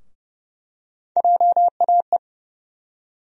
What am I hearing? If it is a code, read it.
Morse code, heard as a keyed tone, about 30 wpm, 700 Hz. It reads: JAE
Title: USA Call Sign Postfixes - 3 Letters 30wpm